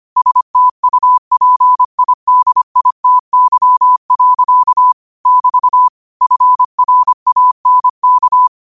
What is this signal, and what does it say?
Morse code, heard as a keyed tone, about 25 wpm, 1000 Hz. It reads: STUPIDITY. <BT> FRANK